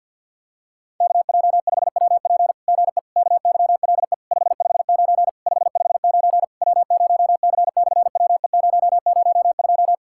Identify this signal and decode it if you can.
Morse code, heard as a keyed tone, about 53 wpm, 700 Hz. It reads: KJ5WP GE KYLE 559 559 W0C/WE002
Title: SOTA QSO Practice 50wpm